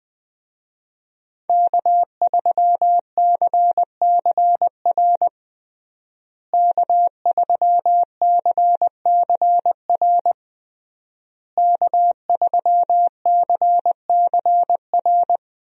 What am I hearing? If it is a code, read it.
Morse code, heard as a keyed tone, about 20 wpm, 700 Hz. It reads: K3CCR K3CCR K3CCR
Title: USA Call Signs - 5 Characters - Repeated 3x - 20wpm